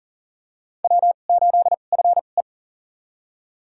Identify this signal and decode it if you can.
Morse code, heard as a keyed tone, about 40 wpm, spaced as for 29 wpm, 700 Hz. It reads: W8FE
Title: USA Call Signs - 4 Characters - 2x Character Spacing - 40wpm